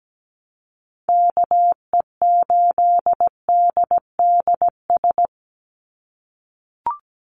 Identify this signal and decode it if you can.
Morse code, heard as a keyed tone, about 17 wpm, 700 Hz. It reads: KE8DDS
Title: USA Call Signs 17wpm